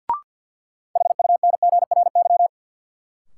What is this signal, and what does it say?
Morse code, heard as a keyed tone, about 52 wpm, 700 Hz. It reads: HUNGRY
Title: Sentences from Top 1400 Words - New Word Drill Alt - 50wpm